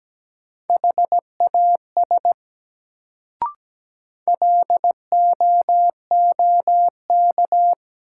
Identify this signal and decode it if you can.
Morse code, heard as a keyed tone, about 17 wpm, 700 Hz. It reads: HAS LOOK
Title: Sets of 2 Words - New Word Drill - 17wpm